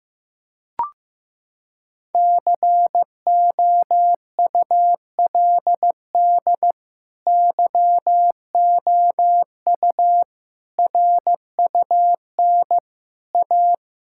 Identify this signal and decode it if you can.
Morse code, heard as a keyed tone, about 15 wpm, 700 Hz. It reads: COULD YOU RUN A
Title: Sentences from Top 1600 Words 15wpm